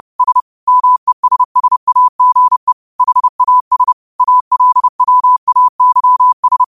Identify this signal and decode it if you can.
Morse code, heard as a keyed tone, about 30 wpm, 1000 Hz. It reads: S MESSAGE HAS ALWAYS
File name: qotdfiles / 30wpm